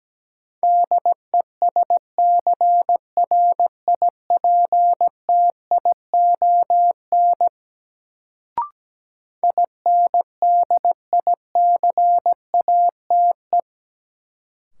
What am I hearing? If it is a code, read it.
Morse code, heard as a keyed tone, about 17 wpm, 700 Hz. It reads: DESCRIPTION INDICATE
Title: Sentences from Top 1300 Words - New Word Drill - 17wpm